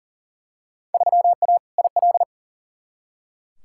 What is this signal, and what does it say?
Morse code, heard as a keyed tone, about 40 wpm, 700 Hz. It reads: 3A IL
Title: Field Day Exchanges - 40wpm